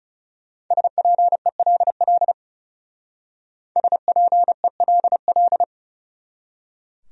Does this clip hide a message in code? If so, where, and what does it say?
Yes: Morse code from 0.7 to 5.6 s, heard as a keyed tone, about 38 wpm, 700 Hz. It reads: SPELL SPELL